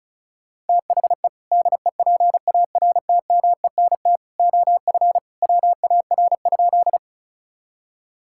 Morse code, heard as a keyed tone, about 34 wpm, 700 Hz. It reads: THE DEPARTMENT OF WAR?
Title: Sentences from Top 1000 Words - Review Part 5 - 35wpm